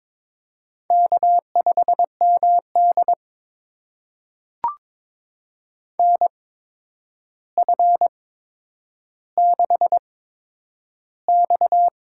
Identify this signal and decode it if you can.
Morse code, heard as a keyed tone, about 22 wpm, spaced as for 9 wpm, 700 Hz. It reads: K5MD NF6X
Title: USA Call Signs - 4 Characters - 8x Character Spacing - 22wpm